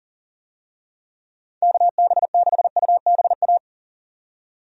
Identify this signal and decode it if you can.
Morse code, heard as a keyed tone, about 37 wpm, 700 Hz. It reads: KB6UBA
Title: USA Call Signs - 6 Characters - 40wpm